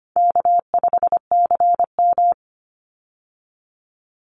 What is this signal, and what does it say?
Morse code, heard as a keyed tone, about 25 wpm, 700 Hz. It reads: K5CM
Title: Call Signs Repeated 3x - 25wpm